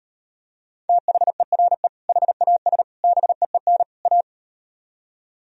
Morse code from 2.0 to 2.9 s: HAS

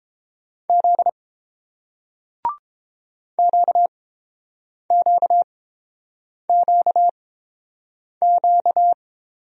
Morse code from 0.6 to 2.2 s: Z

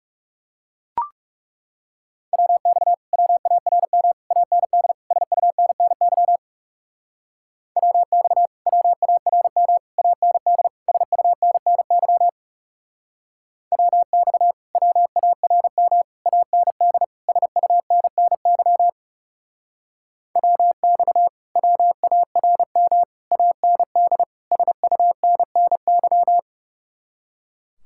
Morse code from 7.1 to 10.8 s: WX WARM AND

